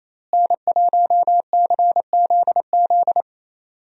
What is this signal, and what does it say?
Morse code, heard as a keyed tone, about 27 wpm, 700 Hz. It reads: N1CZZ
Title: USA Call Signs - 5 Characters - Repeated 3x - 28wpm